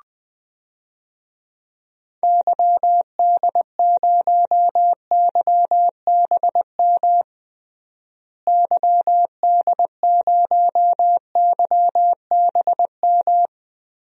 Morse code, heard as a keyed tone, about 20 wpm, 700 Hz. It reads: YD0YBM YD0YBM
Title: DXCC-100 Call Signs - 6 Characters - Repeated 3x - 20wpm